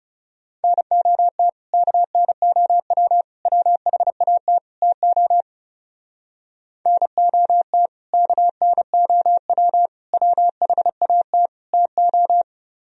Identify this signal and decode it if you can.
Morse code, heard as a keyed tone, about 40 wpm, 700 Hz. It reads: NOT KNOW WHAT TO NOT KNOW WHAT TO